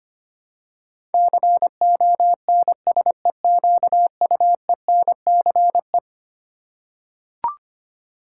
Morse code, heard as a keyed tone, about 25 wpm, 700 Hz. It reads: CONSEQUENCE
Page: morse-code-ninja-practice-25wpm